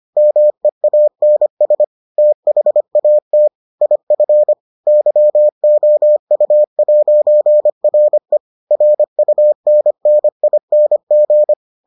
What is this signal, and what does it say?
Morse code, heard as a keyed tone, about 25 wpm, 600 Hz. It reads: MEANS THAT IF YOU'RE RUNNING